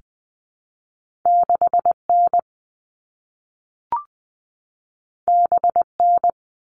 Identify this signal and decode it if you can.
Morse code, heard as a keyed tone, about 20 wpm, 700 Hz. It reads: 6N BN
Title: All Cut Numbers - 2-Digit 20wpm